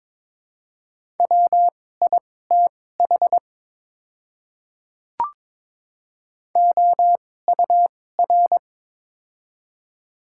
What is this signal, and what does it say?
Morse code, heard as a keyed tone, about 22 wpm, spaced as for 16 wpm, 700 Hz. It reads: WITH OUR
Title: 2x Character Spacing - Top 200 Words - 22wpm